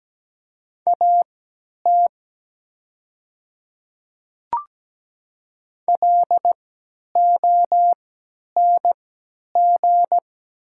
Morse code, heard as a keyed tone, about 17 wpm, spaced as for 10 wpm, 700 Hz. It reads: AT LONG
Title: 3x Character Spacing - Top 200 Words - 17wpm